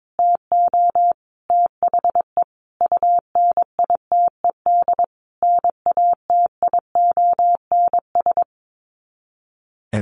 Morse code, heard as a keyed tone, about 22 wpm, 700 Hz. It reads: TO THE UNITED NATIONS
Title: Sets of 5 Words Encore2 22wpm